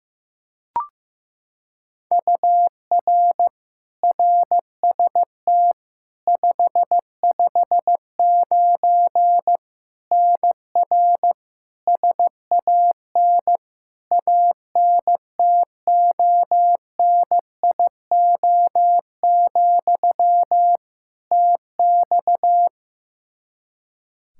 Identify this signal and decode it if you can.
Morse code, heard as a keyed tone, about 15 wpm, 700 Hz. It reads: UR RST 559 NR SAN ANTONIO, TX
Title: K1USN SST QSO Practice 15wpm